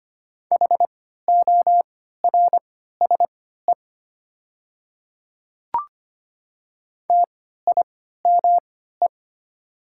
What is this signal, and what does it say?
Morse code, heard as a keyed tone, about 25 wpm, spaced as for 14 wpm, 700 Hz. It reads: HORSE TIME
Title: 3x Character Spacing - Top 300 Words - 25wpm